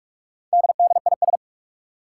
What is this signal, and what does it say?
Morse code, heard as a keyed tone, about 43 wpm, 700 Hz. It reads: DDIS
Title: Four Challenging Letter-Number Combinations 45wpm